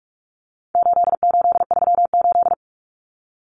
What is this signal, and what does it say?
Morse code, heard as a keyed tone, about 44 wpm, 700 Hz. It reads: 8838
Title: Numbers - 4-Digit 45wpm